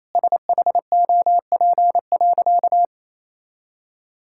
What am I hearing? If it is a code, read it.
Morse code, heard as a keyed tone, about 27 wpm, 700 Hz. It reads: SHOP.
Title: Sentences from Top 1600 Words 28wpm